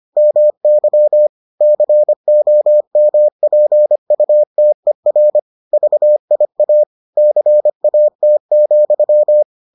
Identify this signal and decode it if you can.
Morse code, heard as a keyed tone, about 25 wpm, 600 Hz. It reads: MY COMPUTER VIA CAT,